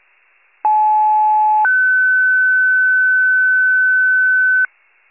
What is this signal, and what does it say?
Two-tone page, heard as a one-second tone, then a three-second tone, about 860 then 1500 Hz. It is heard from 0.6 to 4.7 s.